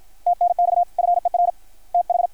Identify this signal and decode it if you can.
Morse code, heard as a keyed tone, about 53 wpm, 660 Hz. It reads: TNX FER TH
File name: archived / CRSnet